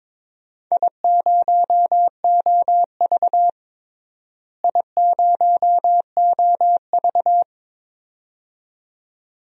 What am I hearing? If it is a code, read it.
Morse code, heard as a keyed tone, about 22 wpm, 700 Hz. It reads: I0OV I0OV
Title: DXCC-100 Call Signs - 4 Characters - Repeated 2x - 22wpm